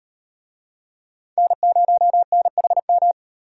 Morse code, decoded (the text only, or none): N0NHM